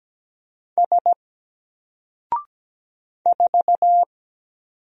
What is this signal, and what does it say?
Morse code, heard as a keyed tone, about 17 wpm, 700 Hz. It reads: S 4
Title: Single Character (Letter:Number:Punctuation:Prosign) Alt 17wpm